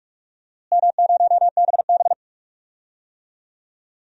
Morse code, heard as a keyed tone, about 46 wpm, 700 Hz. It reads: M0BB
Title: DXCC-100 Call Signs - 4 Characters - Repeated 3x - 45wpm